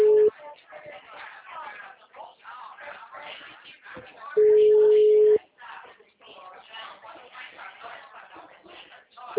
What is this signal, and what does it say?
Ringback tone (Europe, 425 Hz), with speech.
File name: out- -2033- - - .44210